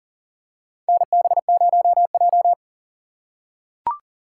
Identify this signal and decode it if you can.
Morse code, heard as a keyed tone, about 40 wpm, 700 Hz. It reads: NB0J
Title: USA Call Signs - 3x Character Spacing - 40wpm